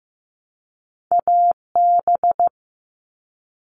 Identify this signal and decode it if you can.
Morse code, heard as a keyed tone, about 15 wpm, 700 Hz. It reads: AB
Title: Canadian Provinces and Territories - Repeated 3x - 15wpm